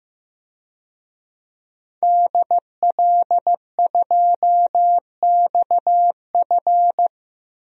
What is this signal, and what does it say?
Morse code, heard as a keyed tone, about 15 wpm, 700 Hz. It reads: DL2XF
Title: DXCC-100 Call Signs - 5 Characters - 15wpm